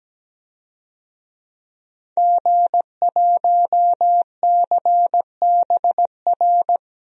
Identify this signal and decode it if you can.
Morse code, heard as a keyed tone, about 17 wpm, 700 Hz. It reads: G1CBR
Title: DXCC-100 Call Signs - 5 Characters - Repeated 3x - 17wpm